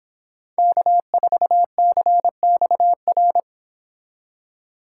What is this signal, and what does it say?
Morse code, heard as a keyed tone, about 31 wpm, 700 Hz. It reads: K4CXR